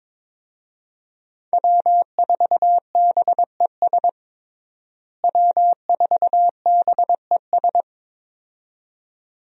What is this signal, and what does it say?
Morse code, heard as a keyed tone, about 22 wpm, 700 Hz. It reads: W4BES W4BES
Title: USA Call Signs - 5 Characters - Repeated 2x - 22wpm